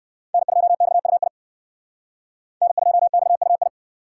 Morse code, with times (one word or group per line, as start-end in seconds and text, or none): 0.3-1.3 N2XUI
2.6-3.7 N2XUI